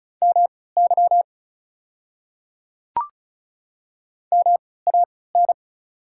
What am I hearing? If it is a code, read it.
Morse code, heard as a keyed tone, about 36 wpm, spaced as for 20 wpm, 700 Hz. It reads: MY MAN
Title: 3x Character Spacing - Top 500 Words - 35wpm